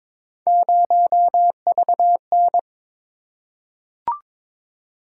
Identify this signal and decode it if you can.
Morse code, heard as a keyed tone, about 22 wpm, 700 Hz. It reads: 0VN
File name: practice / 22wpm / Three Letter-Number Combinations - 22wpm